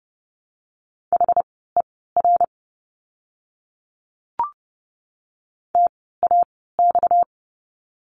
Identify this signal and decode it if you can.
Morse code, heard as a keyed tone, about 30 wpm, spaced as for 17 wpm, 700 Hz. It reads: HER TAX
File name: course / 30:10wpm / 230 Lesson 37f - Letter x - Words 30wpm